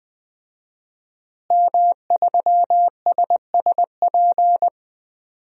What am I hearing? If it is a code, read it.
Morse code, heard as a keyed tone, about 20 wpm, 700 Hz. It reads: M3SSP
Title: DXCC-100 Call Signs - 5 Characters - Repeated 3x - 20wpm